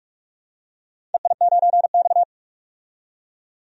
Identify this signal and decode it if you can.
Morse code, heard as a keyed tone, about 46 wpm, 700 Hz. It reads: EI9X
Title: DXCC-100 Call Signs - 4 Characters - 45wpm